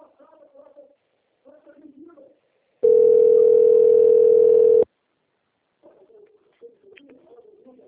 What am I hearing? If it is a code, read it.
Ringback tone (North America), 440 + 480 Hz.
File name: out- - - - - .33196